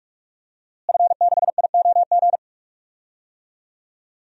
Morse code, heard as a keyed tone, about 46 wpm, 700 Hz. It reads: F6IOG
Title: DXCC-100 Call Signs - 5 Characters - Repeated 2x - 45wpm